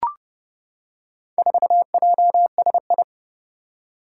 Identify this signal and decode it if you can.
Morse code, heard as a keyed tone, about 30 wpm, 700 Hz. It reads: 4JSI
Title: Four Challenging Letter-Number Combinations 30wpm